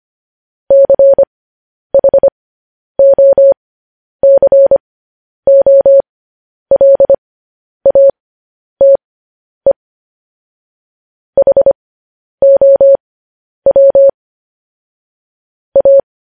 Morse code, heard as a keyed tone, about 25 wpm, spaced as for 10 wpm, 550 Hz. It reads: CHOCOLATE HOW A
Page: news headlines in morse code at 10 wpm